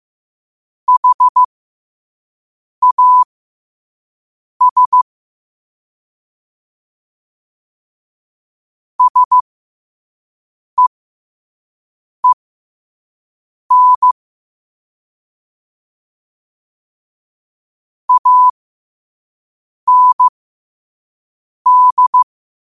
Morse code, heard as a keyed tone, about 15 wpm, spaced as for 5 wpm, 1000 Hz. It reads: HAS SEEN AND
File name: qotdfiles / 5wpm